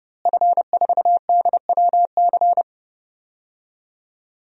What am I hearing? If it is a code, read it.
Morse code, heard as a keyed tone, about 31 wpm, 700 Hz. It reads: F4DWC
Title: DXCC-100 Call Signs - 5 Characters - Repeated 2x - 30wpm